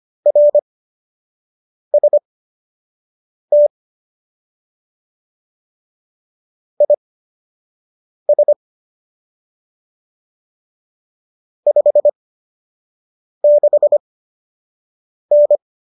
Morse code, heard as a keyed tone, about 25 wpm, spaced as for 6 wpm, 590 Hz. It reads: RST IS 56N